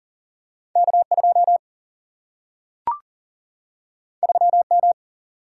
Morse code, heard as a keyed tone, about 40 wpm, 700 Hz. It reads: K 2 3 M
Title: Single-Letter Single-Number 40wpm